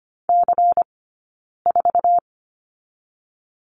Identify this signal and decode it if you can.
Morse code, heard as a keyed tone, about 25 wpm, 700 Hz. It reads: C 4
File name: course / 25:10wpm / 092 Lesson 16d - Letter c - Two Characters 25wpm